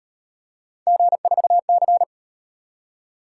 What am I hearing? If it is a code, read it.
Morse code, heard as a keyed tone, about 36 wpm, 700 Hz. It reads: G4C